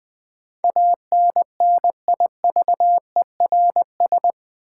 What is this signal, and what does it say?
Morse code, heard as a keyed tone, about 20 wpm, 700 Hz. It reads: ANNIVERS
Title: Build A Word - 4 Letters At A Time - 11-Letter Words - 20wpm